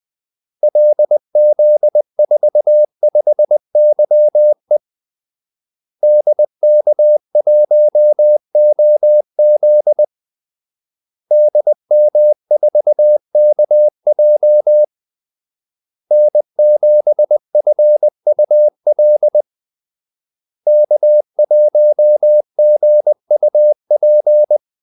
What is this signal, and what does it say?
Morse code, heard as a keyed tone, about 20 wpm, 600 Hz. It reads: LZ45YE DK1OZ DM4KJ N7FUL K1GUP
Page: Kaldesignaler 20 wpm